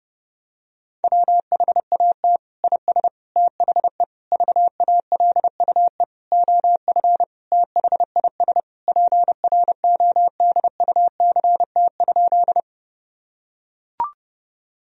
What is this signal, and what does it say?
Morse code, heard as a keyed tone, about 30 wpm, 700 Hz. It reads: WHAT IS THE VALUE OF THIS PRODUCT?
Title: Sentences from Top 1000 Words - Review Part 3 - 30wpm